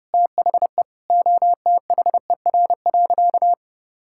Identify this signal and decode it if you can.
Morse code, heard as a keyed tone, about 30 wpm, 700 Hz. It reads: THE OTHER.
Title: Sentences from Top 1000 Words - Review Part 5 - 30wpm